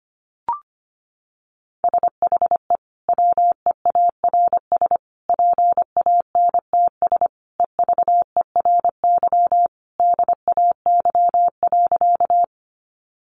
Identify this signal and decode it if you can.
Morse code, heard as a keyed tone, about 25 wpm, 700 Hz. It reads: SHE WEARS PANTS EVERY DAY.
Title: Additional Easy Phrases 25wpm